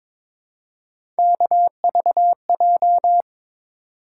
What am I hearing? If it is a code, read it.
Morse code, heard as a keyed tone, about 22 wpm, 700 Hz. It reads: KVJ
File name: practice / 22wpm / Three Letter-Number Combinations - 22wpm